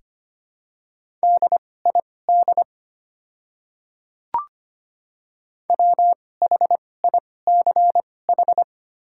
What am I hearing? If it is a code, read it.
Morse code, heard as a keyed tone, about 25 wpm, spaced as for 18 wpm, 690 Hz. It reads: DID WHICH
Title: 2x Character Spacing - Top 200 Words - 25wpm